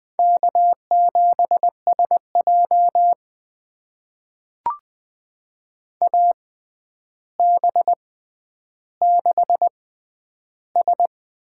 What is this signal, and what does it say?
Morse code, heard as a keyed tone, about 21 wpm, spaced as for 10 wpm, 700 Hz. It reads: K7SJ AB6S
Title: USA Call Signs - 4 Characters - 6x Character Spacing - 20wpm